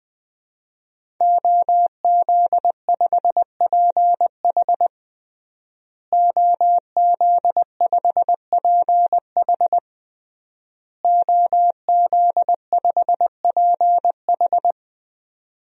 Morse code, heard as a keyed tone, about 20 wpm, 700 Hz. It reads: OZ5PH OZ5PH OZ5PH
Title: DXCC-100 Call Signs - 5 Characters - Repeated 3x - 20wpm